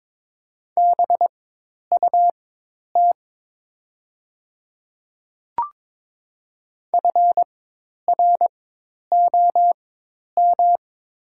Morse code, heard as a keyed tone, about 22 wpm, spaced as for 10 wpm, 700 Hz. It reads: BUT FROM
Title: 4x Character Spacing - Top 100 Words - 22wpm